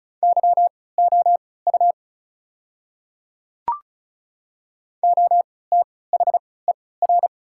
Morse code, heard as a keyed tone, about 36 wpm, spaced as for 20 wpm, 700 Hz. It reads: YOU OTHER
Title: 3x Character Spacing - Top 100 Words - 35wpm